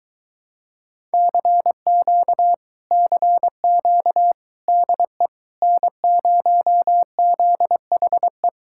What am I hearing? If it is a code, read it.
Morse code, heard as a keyed tone, about 22 wpm, 700 Hz. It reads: CQ CQ DE N0ZHE